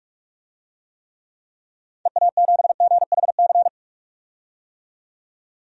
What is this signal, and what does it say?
Morse code, heard as a keyed tone, about 46 wpm, 700 Hz. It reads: EA7GHC
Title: DXCC-100 Call Signs - 6 Characters - 45wpm